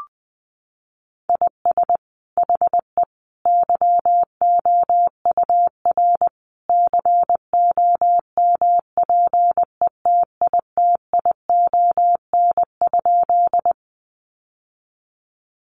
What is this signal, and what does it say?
Morse code, heard as a keyed tone, about 19 wpm, 700 Hz. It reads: IS HE YOUR COMPETITION?
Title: Sentences from Top 1000 Words 20wpm